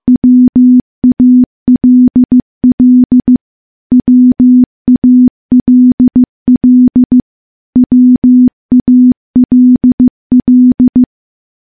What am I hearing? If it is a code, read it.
Morse code, heard as a keyed tone, about 15 wpm, 250 Hz. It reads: WALL WALL WALL
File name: morse-training-15WPM